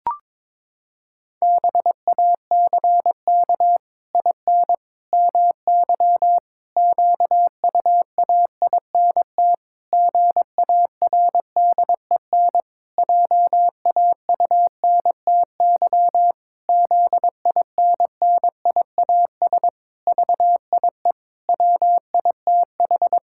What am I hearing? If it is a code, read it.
Morse code, heard as a keyed tone, about 22 wpm, 700 Hz. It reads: BACK IN MY QUAINT GARDEN JAUNTY ZINNIAS VIE WITH
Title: Pangram Challenge v2 - 22wpm